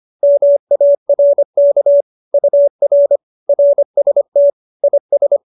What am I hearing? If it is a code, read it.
Morse code, heard as a keyed tone, about 25 wpm, 570 Hz. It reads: MARK UR RST IS
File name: QSO306 25